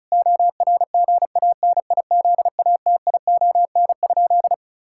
Morse code, heard as a keyed tone, about 36 wpm, 700 Hz. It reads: ORGANIZATION?
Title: Sentences from Top 1000 Words - Review Part 5 - 35wpm